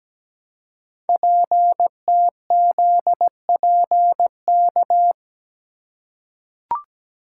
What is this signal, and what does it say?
Morse code, heard as a keyed tone, about 17 wpm, 700 Hz. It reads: PTZPK
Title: Code Groups - 5 Letters - 17wpm